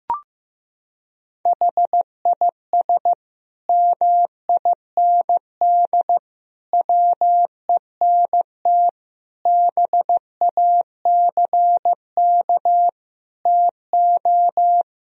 Morse code, heard as a keyed tone, about 15 wpm, 700 Hz. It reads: HIS MIND WENT BACK TO
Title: Sentences from Top 1000 Words - Review Part 3 - 15wpm